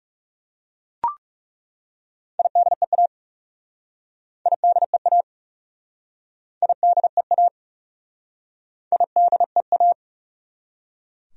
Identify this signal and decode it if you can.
Morse code, heard as a keyed tone, about 40 wpm, 700 Hz. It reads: IDEA IDEA IDEA IDEA